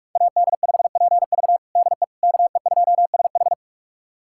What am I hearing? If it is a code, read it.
Morse code, heard as a keyed tone, about 48 wpm, 700 Hz. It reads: AD5PV DE KE2SH
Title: USA Callsign DE Callsign - 5 Characters - 45wpm